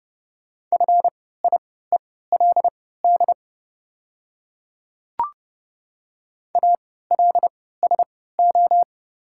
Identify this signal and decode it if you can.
Morse code, heard as a keyed tone, about 30 wpm, spaced as for 17 wpm, 700 Hz. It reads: FIELD ALSO
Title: 3x Character Spacing - Top 500 Words - 30wpm